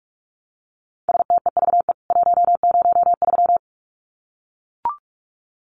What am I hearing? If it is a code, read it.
Morse code, heard as a keyed tone, about 44 wpm, 700 Hz. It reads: STEVE 103
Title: CWT Exchange 45wpm